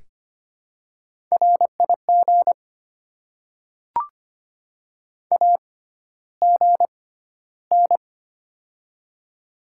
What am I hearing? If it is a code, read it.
Morse code, heard as a keyed tone, about 25 wpm, spaced as for 13 wpm, 700 Hz. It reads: RIG AGN
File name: Top 20 QSO Elements - 6x Character Spacing - 25wpm